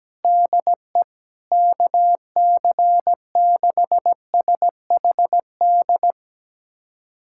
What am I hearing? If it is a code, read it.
Morse code, heard as a keyed tone, about 17 wpm, 700 Hz. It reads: DE KC6SHD
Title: USA Callsign DE Callsign 17wpm